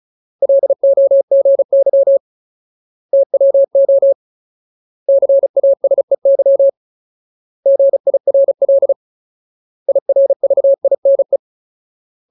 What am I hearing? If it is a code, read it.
Morse code, heard as a keyed tone, about 34 wpm, 550 Hz. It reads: LOGY TWO CASEY GIRL IRVINE
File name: Ultimate Word List / 35WPM Ultimate